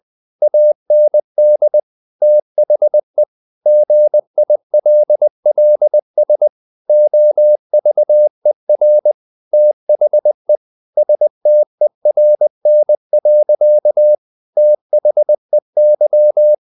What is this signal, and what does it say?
Morse code, heard as a keyed tone, about 20 wpm, 600 Hz. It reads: AND THE GILLS OVER THE STERN. THEY
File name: the old man and the sea part 2 20wpm